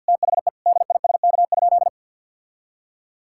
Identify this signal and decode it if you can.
Morse code, heard as a keyed tone, about 50 wpm, 700 Hz. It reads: THE DISK?
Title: Sentences from Top 1000 Words - Review Part 6 - 50wpm